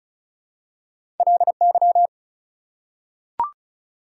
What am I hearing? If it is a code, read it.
Morse code, heard as a keyed tone, about 34 wpm, 700 Hz. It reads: LY
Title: Prefixes and Suffixes 35wpm